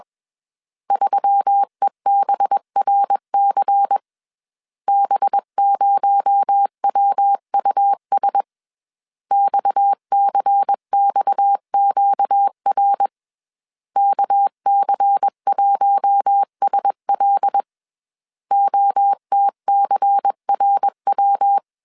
Morse code, heard as a keyed tone, about 21 wpm, 800 Hz. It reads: 3EBRC B0WUS XCXQR KC1SL OTCRW
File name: CW / DET / 1 6 1500C S B 20wpm 2048